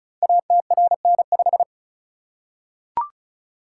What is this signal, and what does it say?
Morse code, heard as a keyed tone, about 36 wpm, 690 Hz. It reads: ATRN5
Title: Code Groups - 5 Letter:Numbers - 35wpm